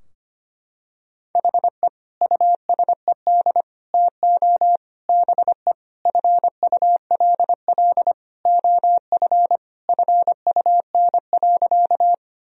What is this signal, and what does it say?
Morse code, heard as a keyed tone, about 25 wpm, 700 Hz. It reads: HE USED TO BE FULL OF FUN.
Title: Sentences from Top 1000 Words - Review Part 4 - 25wpm